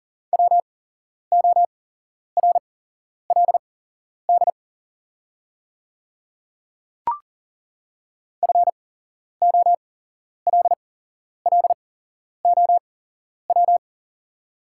Morse code, heard as a keyed tone, about 41 wpm, spaced as for 11 wpm, 690 Hz. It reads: WORLD FOLLOW
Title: 8x Character Spacing - Top 300 Words - 40wpm